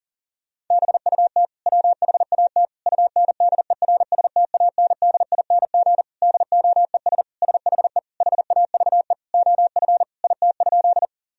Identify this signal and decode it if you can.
Morse code, heard as a keyed tone, about 40 wpm, 700 Hz. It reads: BUT WHAT UNDERSTANDING DOES SHE HAVE OF IT?
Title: Sentences from Top 1000 Words - Review Part 3 - 40wpm